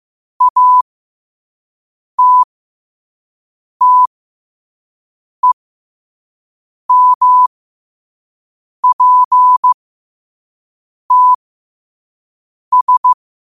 Morse code, heard as a keyed tone, about 15 wpm, spaced as for 5 wpm, 1000 Hz. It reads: ATTEMPTS
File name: qotdfiles / 5wpm